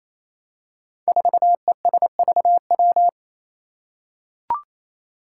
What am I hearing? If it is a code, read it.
Morse code, heard as a keyed tone, about 28 wpm, 700 Hz. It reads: 4ESVW
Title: Five Challenging Letter-Number Combinations 28wpm